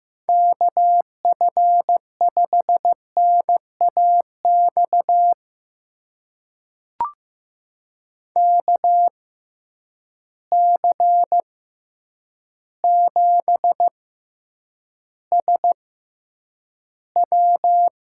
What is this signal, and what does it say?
Morse code, heard as a keyed tone, about 15 wpm, spaced as for 8 wpm, 700 Hz. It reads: KF5NAX KC7SW
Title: USA Call Signs - 6 Characters - 6x Character Spacing - 15wpm